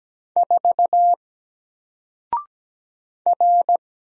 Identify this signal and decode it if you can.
Morse code, heard as a keyed tone, about 17 wpm, 700 Hz. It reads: 4 R
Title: Single Letter-Number Alt 17wpm